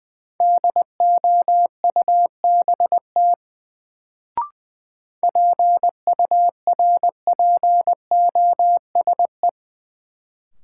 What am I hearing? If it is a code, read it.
Morse code, heard as a keyed tone, about 20 wpm, 700 Hz. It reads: DOUBT PURPOSE
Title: Sets of 5 Words - New Word Drill - 20wpm